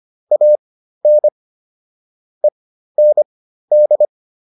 Morse code, heard as a keyed tone, about 25 wpm, spaced as for 13 wpm, 600 Hz. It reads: AN END
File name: WD404 13